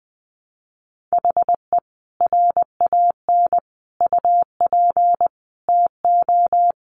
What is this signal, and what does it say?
Morse code, heard as a keyed tone, about 20 wpm, 700 Hz. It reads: HE RAN UP TO
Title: Sentences from Top 500 Words 20wpm